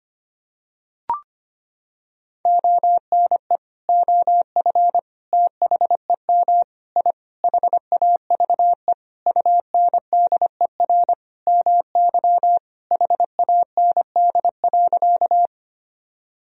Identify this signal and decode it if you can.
Morse code, heard as a keyed tone, about 25 wpm, 700 Hz. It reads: ONE OF THEM I HAVE UNDER MY HAND.
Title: Sentences from Top 1000 Words - Review Part 1 - 25wpm